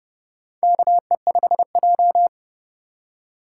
Morse code, heard as a keyed tone, about 29 wpm, 700 Hz. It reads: KE5J
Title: USA Call Signs - 4 Characters - 3x Character Spacing - 30wpm